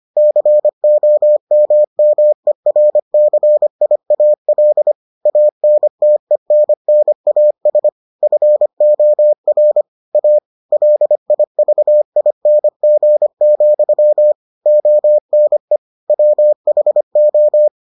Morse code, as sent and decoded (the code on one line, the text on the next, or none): -.-. --- -- -- . .-. -.-. .. .- .-.. / .- -. - . -. -. .- ... / ..-. --- .-. / .- / .-.. .. ...- .. -. --. --..-- / --- -. / .-- .... ---
COMMERCIAL ANTENNAS FOR A LIVING, ONE WHO